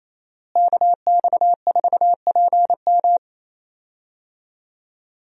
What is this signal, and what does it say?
Morse code, heard as a keyed tone, about 28 wpm, 700 Hz. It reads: KX4PM
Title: USA Call Signs - 5 Characters - Repeated 3x - 28wpm